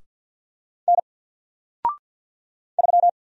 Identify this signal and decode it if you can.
Morse code, heard as a keyed tone, about 50 wpm, 700 Hz. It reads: N 3
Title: Single Letter-Number Alt 50wpm